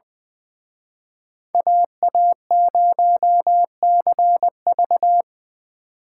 Morse code, heard as a keyed tone, about 23 wpm, 700 Hz. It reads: AA0CV
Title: Speed Racing - USA 2x2 Call Signs 20wpm